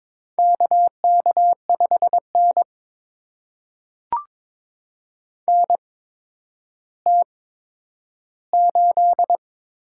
Morse code, heard as a keyed tone, about 22 wpm, spaced as for 11 wpm, 700 Hz. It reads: KK5N NT8
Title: USA Call Signs - 4 Characters - 8x Character Spacing - 22wpm